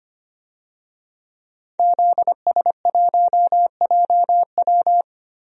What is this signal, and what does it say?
Morse code, heard as a keyed tone, about 25 wpm, 700 Hz. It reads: ZS1JW
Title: DXCC-100 Call Signs - 5 Characters - Repeated 3x - 25wpm